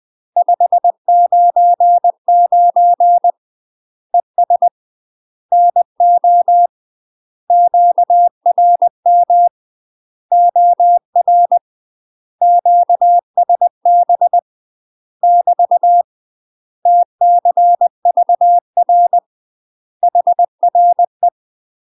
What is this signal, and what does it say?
Morse code, heard as a keyed tone, about 20 wpm, 700 Hz. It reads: 599 ES NO QRM OR QSB <BT> TCVR HRE
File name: UK MASTER QSOs TRAINING / 20WPM / UK MASTER QSOs 0001